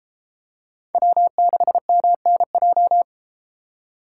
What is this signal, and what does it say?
Morse code, heard as a keyed tone, about 33 wpm, 700 Hz. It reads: W6MNJ